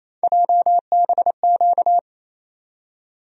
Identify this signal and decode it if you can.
Morse code, heard as a keyed tone, about 28 wpm, 700 Hz. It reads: JBQ
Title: USA Call Sign Postfixes - 3 Letters 28wpm